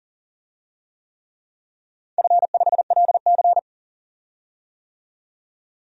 Morse code, heard as a keyed tone, about 41 wpm, 700 Hz. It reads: F5LC
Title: DXCC-100 Call Signs - 4 Characters - 40wpm